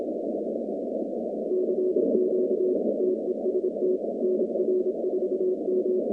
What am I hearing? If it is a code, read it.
Morse code, heard as a keyed tone, about 27 wpm, 370 Hz, with noise. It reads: CQ TEST NR4M
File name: contests / 2019-cw-wpx-kt5j-k5pi- -2330 0-47